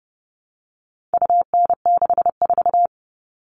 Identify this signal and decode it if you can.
Morse code, heard as a keyed tone, about 29 wpm, 700 Hz. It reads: UN64
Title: All Cut Numbers - 4-Digit 30wpm